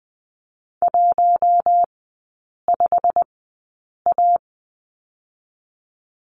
Morse code, heard as a keyed tone, about 19 wpm, spaced as for 8 wpm, 700 Hz. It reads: 15A
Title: 127 Lesson 21e - Number 6 - Three Characters 20wpm